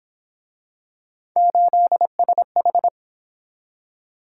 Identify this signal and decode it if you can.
Morse code, heard as a keyed tone, about 26 wpm, 700 Hz. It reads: 8SH
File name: Speed-Racing - Three Letter-Number Combinations - 22wpm